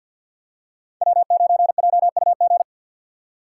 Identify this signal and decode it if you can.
Morse code, heard as a keyed tone, about 47 wpm, 700 Hz. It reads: W9JUG
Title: K1USN SST QSO Practice - 50wpm